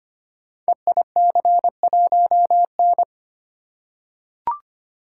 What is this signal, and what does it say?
Morse code, heard as a keyed tone, about 25 wpm, 700 Hz. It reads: EIC1N
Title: Code Groups - 5 Letter:Numbers - 25wpm